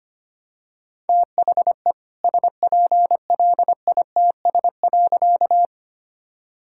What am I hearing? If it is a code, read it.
Morse code, heard as a keyed tone, about 25 wpm, 700 Hz. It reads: THE SPLITS.
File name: practice / 25wpm / All-Age Appropriate Jokes - 25wpm